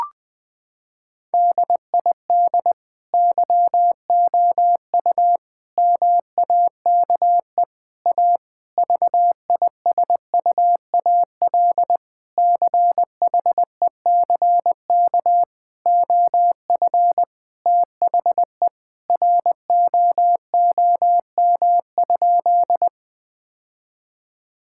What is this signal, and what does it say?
Morse code, heard as a keyed tone, about 20 wpm, 700 Hz. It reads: DID YOU MAKE A VISUAL CHECK OF THE ROOM?
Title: Sentences from Top 1000 Words - Review Part 6 - 20wpm